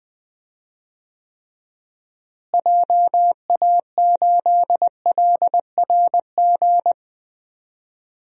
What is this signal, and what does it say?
Morse code, heard as a keyed tone, about 20 wpm, 700 Hz. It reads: JA8LRG
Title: DXCC-100 Call Signs - 6 Characters - Repeated 3x - 20wpm